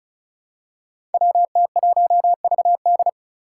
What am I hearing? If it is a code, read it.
Morse code, heard as a keyed tone, about 36 wpm, 700 Hz. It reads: WT1VD
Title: Five Challenging Letter-Number Combinations 35wpm